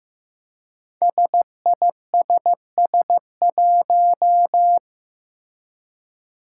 Morse code, heard as a keyed tone, about 15 wpm, 700 Hz. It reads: SISS1
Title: Five Challenging Letter-Number Combinations 15wpm